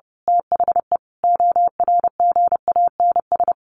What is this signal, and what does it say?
Morse code, heard as a keyed tone, about 30 wpm, 700 Hz. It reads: THE ORGANS
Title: A Sound Of Thunder 30wpm